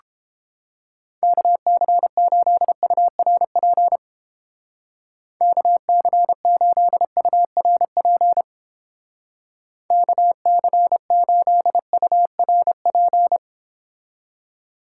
Morse code, heard as a keyed tone, about 31 wpm, 700 Hz. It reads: KC8URP KC8URP KC8URP